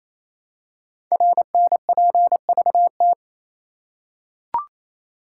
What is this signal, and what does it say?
Morse code, heard as a keyed tone, about 27 wpm, 700 Hz. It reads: RNPVT
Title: Code Groups - 5 Letters - 28wpm